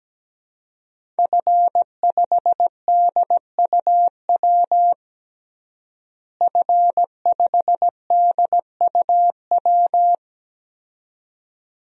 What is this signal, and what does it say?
Morse code, heard as a keyed tone, about 17 wpm, 700 Hz. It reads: F5DUW F5DUW
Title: DXCC-100 Call Signs - 5 Characters - Repeated 2x - 17wpm